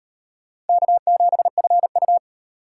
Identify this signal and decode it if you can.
Morse code, heard as a keyed tone, about 38 wpm, 700 Hz. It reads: K7FU